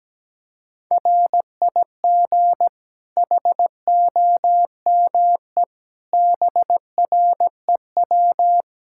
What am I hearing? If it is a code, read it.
Morse code, heard as a keyed tone, about 17 wpm, 700 Hz. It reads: RIG HOME BREW
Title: Practice QSO - Exchange 2 - Info - 17wpm